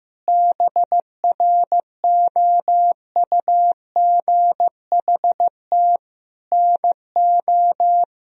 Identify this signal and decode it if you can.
Morse code, heard as a keyed tone, about 16 wpm, 700 Hz. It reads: BROUGHT NO
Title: Sentences from Top 500 Words and Abbreviations 15wpm